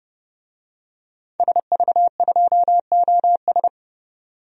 Morse code, heard as a keyed tone, about 30 wpm, 700 Hz. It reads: SV2OS
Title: DXCC-100 Call Signs - 5 Characters - 30wpm